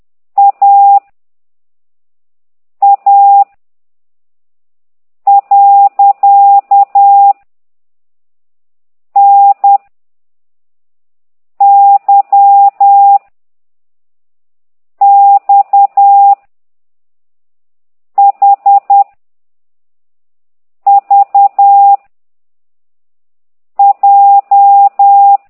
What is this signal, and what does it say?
Morse code, heard as a keyed tone, about 10 wpm, 800 Hz. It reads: A A . N Y X H V J
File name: exerc7